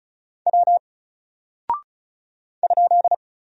Morse code, heard as a keyed tone, about 34 wpm, 700 Hz. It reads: W ?
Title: Single Character (Letter:Number:Punctuation:Prosign) Alt 35wpm